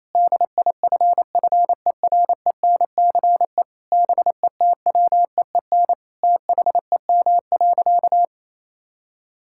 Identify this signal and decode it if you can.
Morse code, heard as a keyed tone, about 28 wpm, 700 Hz. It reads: DIFFERENCE BETWEEN THEM.
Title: Sentences from Top 1000 Words - Review Part 5 - 28wpm